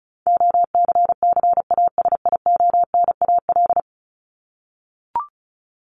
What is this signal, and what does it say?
Morse code, heard as a keyed tone, about 34 wpm, 700 Hz. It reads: OCCASIONAL
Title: LY Suffix - Top 100 Words with Base Words 35wpm